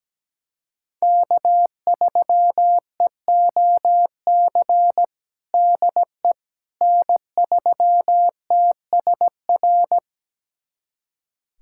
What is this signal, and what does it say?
Morse code, heard as a keyed tone, about 17 wpm, 700 Hz. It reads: K3EOC DE N3TSR
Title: USA Callsign DE Callsign - 5 Characters - 17wpm